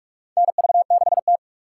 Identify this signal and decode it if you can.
Morse code, heard as a keyed tone, about 46 wpm, 700 Hz. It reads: NV6T
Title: USA Call Signs - 4x Character Spacing - 45wpm